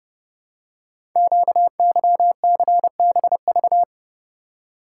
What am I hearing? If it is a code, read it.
Morse code, heard as a keyed tone, about 30 wpm, 700 Hz. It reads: QYCBV
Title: Code Groups - 5 Letters - 30wpm